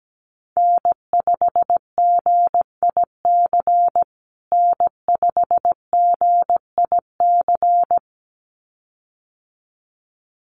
Morse code, heard as a keyed tone, about 17 wpm, 700 Hz. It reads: N5GIC N5GIC
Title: USA Call Signs Repeated 2x 17wpm